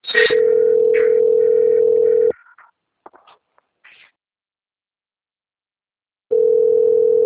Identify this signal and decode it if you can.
Ringback tone (North America), 440 + 480 Hz.